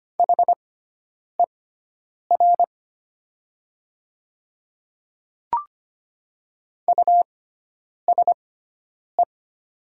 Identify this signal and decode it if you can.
Morse code, heard as a keyed tone, about 25 wpm, spaced as for 9 wpm, 700 Hz. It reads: HER USE
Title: 6x Character Spacing - Top 100 Words - 25wpm